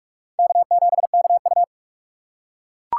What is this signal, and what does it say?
Morse code, heard as a keyed tone, about 41 wpm, 700 Hz. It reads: K7KU